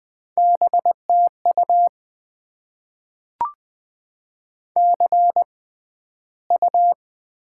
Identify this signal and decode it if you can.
Morse code, heard as a keyed tone, about 20 wpm, spaced as for 12 wpm, 700 Hz. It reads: BTU CU
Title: Top 20 QSO Elements - 6x Character Spacing - 20wpm